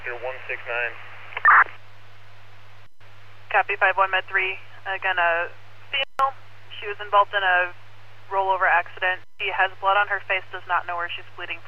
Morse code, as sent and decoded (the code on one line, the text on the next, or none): none